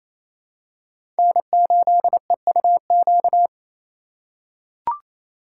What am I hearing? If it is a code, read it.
Morse code, heard as a keyed tone, about 27 wpm, 700 Hz. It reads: N8EUQ
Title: USA Call Signs - 5 Characters - 4x Character Spacing - 28wpm